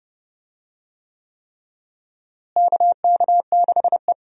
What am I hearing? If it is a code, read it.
Morse code, heard as a keyed tone, about 29 wpm, 700 Hz. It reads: KK6E
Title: USA Call Signs - 4 Characters - Repeated 3x - 30wpm